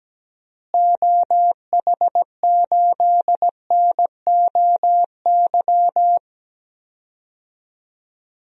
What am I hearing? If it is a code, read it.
Morse code, heard as a keyed tone, about 17 wpm, 700 Hz. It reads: OH8NOY
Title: DXCC-100 Call Signs - 6 Characters - Repeated 3x - 17wpm